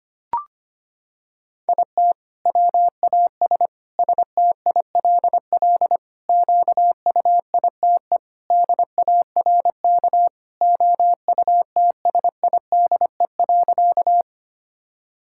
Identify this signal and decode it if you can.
Morse code, heard as a keyed tone, about 25 wpm, 700 Hz. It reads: IT WAS STILL QUITE DARK OUTSIDE.
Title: Sentences from Top 1000 Words - Review Part 6 - 25wpm